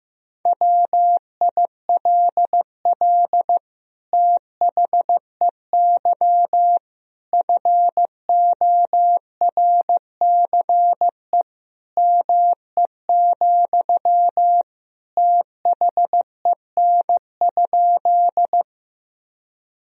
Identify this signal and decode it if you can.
Morse code, heard as a keyed tone, about 15 wpm, 700 Hz. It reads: WILL THEY FORCE ME, THEN?
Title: Sentences from Top 1000 Words - Review Part 3 - 15wpm